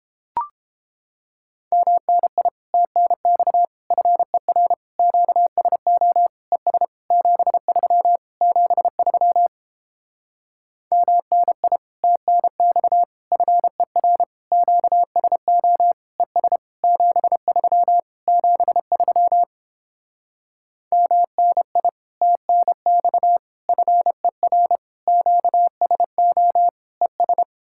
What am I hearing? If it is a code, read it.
Morse code, heard as a keyed tone, about 31 wpm, 700 Hz. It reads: MNI TNX FER QSO ES 73 73 MNI TNX FER QSO ES 73 73 MNI TNX FER QSO ES